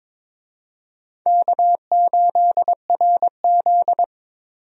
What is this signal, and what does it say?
Morse code, heard as a keyed tone, about 22 wpm, 700 Hz. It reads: K8RZ
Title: USA Call Signs - 4 Characters - Repeated 2x - 22wpm